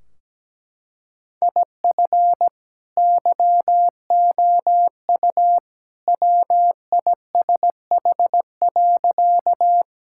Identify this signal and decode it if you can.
Morse code, heard as a keyed tone, about 17 wpm, 700 Hz. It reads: IF YOU WISH.
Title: Sentences from Top 1600 Words 17wpm